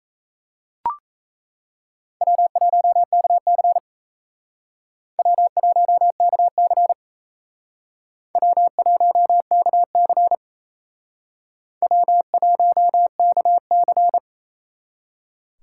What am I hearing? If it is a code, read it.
Morse code, heard as a keyed tone, about 36 wpm, 700 Hz. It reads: W1KC W1KC W1KC W1KC